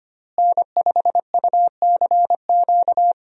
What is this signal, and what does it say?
Morse code, heard as a keyed tone, about 25 wpm, 700 Hz. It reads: N5UCQ
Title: USA Call Signs - 5 Characters - Repeated 3x - 25wpm